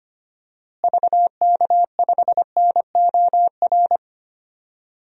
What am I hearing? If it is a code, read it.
Morse code, heard as a keyed tone, about 25 wpm, 700 Hz. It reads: VK5NOR
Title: DXCC-100 Call Signs - 6 Characters - Repeated 3x - 25wpm